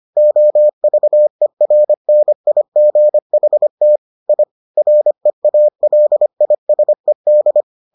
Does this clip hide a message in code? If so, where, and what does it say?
Yes: Morse code from 0.2 to 7.6 s, heard as a keyed tone, about 25 wpm, 600 Hz. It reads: OVERNIGHT I REALISED